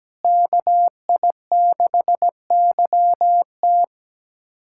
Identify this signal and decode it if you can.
Morse code, heard as a keyed tone, about 17 wpm, 700 Hz. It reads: KI6YT
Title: USA Callsign DE Callsign - 5 Characters - 17wpm